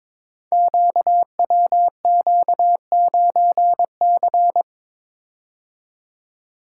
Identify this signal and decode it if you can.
Morse code, heard as a keyed tone, about 23 wpm, 700 Hz. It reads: QWQ9C
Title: Code Groups - 5 Letter:Numbers - 22wpm